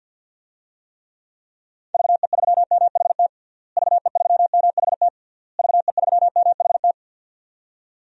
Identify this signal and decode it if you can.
Morse code, heard as a keyed tone, about 49 wpm, 700 Hz. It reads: VE3MHT VE3MHT VE3MHT
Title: DXCC-100 Call Signs - 6 Characters - Repeated 3x - 50wpm